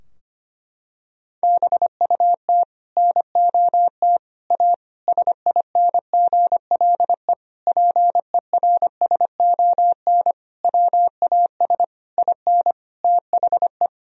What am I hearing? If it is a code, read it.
Morse code, heard as a keyed tone, about 25 wpm, 690 Hz. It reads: BUT NOT A SINGLE PERSON WAS IN THE
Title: Sentences from Top 1000 Words - Review Part 4 - 25wpm